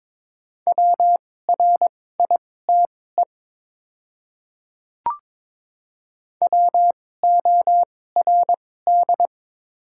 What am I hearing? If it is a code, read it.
Morse code, heard as a keyed tone, about 21 wpm, spaced as for 16 wpm, 700 Hz. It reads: WRITE WORD
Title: 2x Character Spacing - Top 200 Words - 22wpm